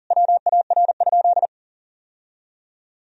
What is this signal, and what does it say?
Morse code, heard as a keyed tone, about 40 wpm, 700 Hz. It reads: WAR?
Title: Sentences from Top 1000 Words - Review Part 2 - 40wpm